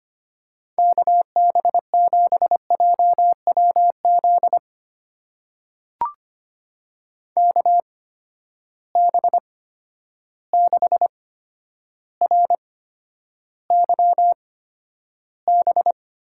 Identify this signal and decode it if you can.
Morse code, heard as a keyed tone, about 25 wpm, spaced as for 11 wpm, 700 Hz. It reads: KB7JWZ KB6RYB
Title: USA Call Signs - 6 Characters - 8x Character Spacing - 25wpm